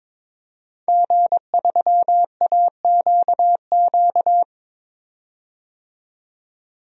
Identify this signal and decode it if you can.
Morse code, heard as a keyed tone, about 22 wpm, 700 Hz. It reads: G3AQQ
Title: DXCC-100 Call Signs - 5 Characters - 22wpm